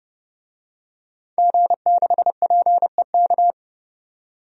Morse code, heard as a keyed tone, about 30 wpm, 700 Hz. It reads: G6PEK
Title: DXCC-100 Call Signs - 5 Characters - 30wpm